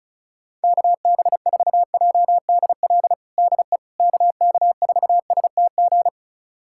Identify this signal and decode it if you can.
Morse code, heard as a keyed tone, about 36 wpm, 700 Hz. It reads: KB4JDL DE KK4STG
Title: USA Callsign DE Callsign 35wpm